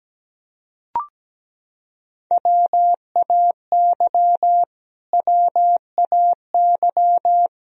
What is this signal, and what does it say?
Morse code, heard as a keyed tone, about 17 wpm, 700 Hz. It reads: WAY WAY
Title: Top 100 Words Repeated 17wpm